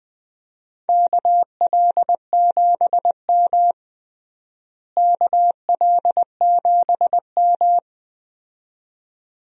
Morse code, heard as a keyed tone, about 20 wpm, 700 Hz. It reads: KL7M KL7M
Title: DXCC-100 Call Signs - 4 Characters - Repeated 2x - 20wpm